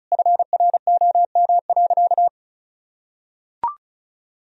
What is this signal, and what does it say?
Morse code, heard as a keyed tone, about 36 wpm, 700 Hz. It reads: FROM.
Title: Sentences from Top 1000 Words - Review Part 4 - 35wpm